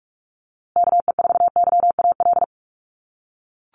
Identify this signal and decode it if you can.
Morse code, heard as a keyed tone, about 44 wpm, 700 Hz. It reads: KE4YAL
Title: USA Call Signs Repeated 3x 45wpm